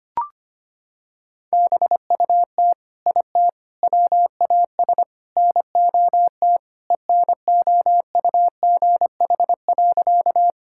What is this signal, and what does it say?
Morse code, heard as a keyed tone, about 25 wpm, 700 Hz. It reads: BUT IT WAS NOT ENOUGH.
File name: practice / 25wpm / Sentences with 4-Word Phrases - 25wpm